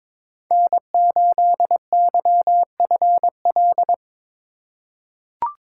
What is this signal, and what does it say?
Morse code, heard as a keyed tone, about 22 wpm, 700 Hz. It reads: N8YFL
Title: USA Call Signs - 5 Characters - 4x Character Spacing - 22wpm